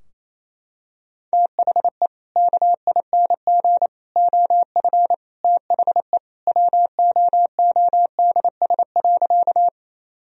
Morse code, heard as a keyed tone, about 28 wpm, 700 Hz. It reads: THE KING OF THE WOODS.
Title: Sentences from Top 1000 Words - Review Part 2 - 28wpm